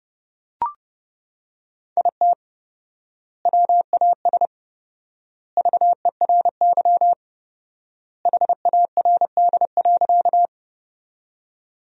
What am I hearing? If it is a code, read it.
Morse code, heard as a keyed tone, about 30 wpm, 700 Hz. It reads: IT WAS VERY HARD.
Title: 4x Word-Spacing - Sentences from Top 300 Words - 30wpm